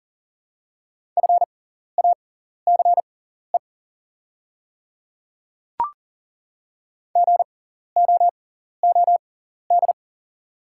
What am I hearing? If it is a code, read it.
Morse code, heard as a keyed tone, about 40 wpm, spaced as for 14 wpm, 690 Hz. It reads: FACE GOOD
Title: 6x Character Spacing - Top 300 Words - 40wpm